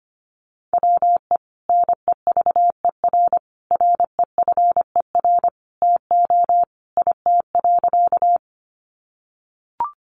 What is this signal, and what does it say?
Morse code, heard as a keyed tone, about 25 wpm, 700 Hz. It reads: WE NEVER REFER TO IT.
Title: Sentences from Top 1000 Words 25wpm